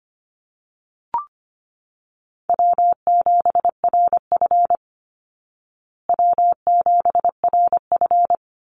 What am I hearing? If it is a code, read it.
Morse code, heard as a keyed tone, about 25 wpm, 700 Hz. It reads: W7RF W7RF
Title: Easy Call Signs Repeated 3x - 25wpm